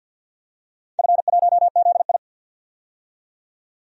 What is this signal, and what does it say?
Morse code, heard as a keyed tone, about 49 wpm, 700 Hz. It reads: F1ZI
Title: DXCC-100 Call Signs - 4 Characters - 50wpm